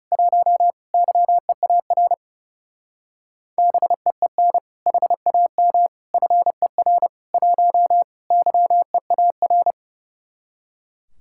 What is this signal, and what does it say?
Morse code, heard as a keyed tone, about 37 wpm, 700 Hz. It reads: 1 YEAR BEEN HAM FER 1 YEAR
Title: Speed-Racing Common QSO Phrases - 30wpm